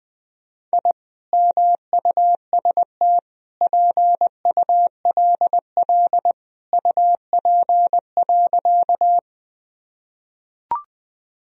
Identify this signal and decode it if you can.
Morse code, heard as a keyed tone, about 21 wpm, 700 Hz. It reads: I MUST PULL UP.
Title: Sentences from Top 1000 Words - Review Part 2 - 20wpm